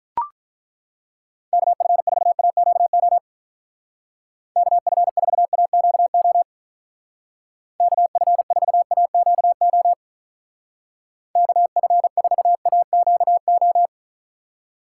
Morse code, heard as a keyed tone, about 42 wpm, 700 Hz. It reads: KF4AQO KF4AQO KF4AQO KF4AQO
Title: Speed Racing - USA 2x3 Call Signs 35wpm